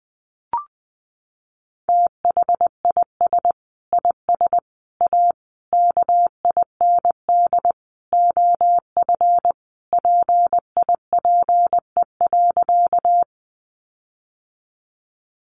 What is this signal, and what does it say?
Morse code, heard as a keyed tone, about 20 wpm, 700 Hz. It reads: THIS IS A KIND OF PIPE.
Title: Sentences from Top 1500 Words 20wpm